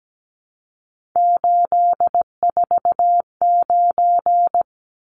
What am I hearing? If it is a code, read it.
Morse code, heard as a keyed tone, about 17 wpm, 700 Hz. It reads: 849
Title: Numbers - 3-Digit 17wpm